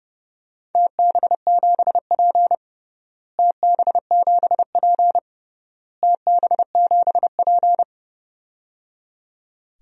Morse code, heard as a keyed tone, about 29 wpm, 700 Hz. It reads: TB7P TB7P TB7P
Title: DXCC-100 Call Signs - 4 Characters - Repeated 3x - 30wpm